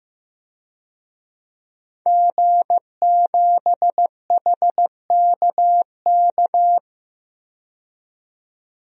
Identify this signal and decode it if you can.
Morse code, heard as a keyed tone, about 15 wpm, 700 Hz. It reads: G7HKK
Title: DXCC-100 Call Signs - 5 Characters - 15wpm